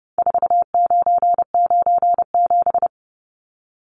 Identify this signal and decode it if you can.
Morse code, heard as a keyed tone, about 30 wpm, 700 Hz. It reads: 4997
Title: Numbers - 4-Digit 30wpm